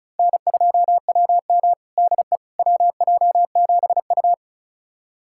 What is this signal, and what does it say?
Morse code, heard as a keyed tone, about 36 wpm, 700 Hz. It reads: N2WM DE WJ7U
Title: USA Callsign DE Callsign - 4 Characters - 35wpm